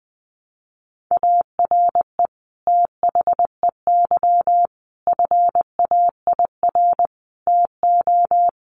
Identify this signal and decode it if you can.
Morse code, heard as a keyed tone, about 20 wpm, 700 Hz. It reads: ARE THEY FAIR TO
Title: Sentences from Top 1300 Words 20wpm